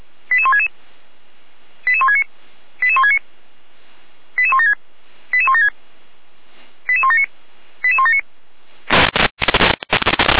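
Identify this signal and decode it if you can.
Five-tone selective call (ZVEI), heard as quick runs of five tones, tones of about 70 ms. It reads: 88166 88169 88169 88176 88176 88179 88179
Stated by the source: ZVEI five-tone selective call 88413 received on an SDR.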